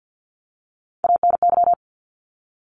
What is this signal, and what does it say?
Morse code, heard as a keyed tone, about 50 wpm, 700 Hz. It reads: ANY
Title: Top Three Letter Combinations in Words 50wpm